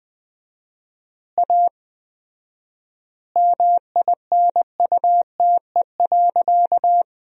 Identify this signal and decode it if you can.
Morse code, heard as a keyed tone, about 20 wpm, 700 Hz. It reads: A MINUTE.
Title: 4x Word-Spacing - Sentences from Top 500 Words - 20wpm